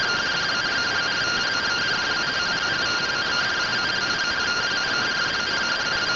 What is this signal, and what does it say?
Two-tone teleprinter signal, about 170 Hz shift.